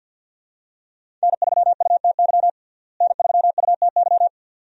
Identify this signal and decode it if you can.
Morse code, heard as a keyed tone, about 46 wpm, 700 Hz. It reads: N3UTY N3UTY
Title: USA Call Signs - 5 Characters - Repeated 2x - 50wpm